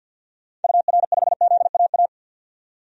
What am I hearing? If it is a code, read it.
Morse code, heard as a keyed tone, about 53 wpm, 700 Hz. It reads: UR5ZAA
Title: DXCC-100 Call Signs - 6 Characters - 50wpm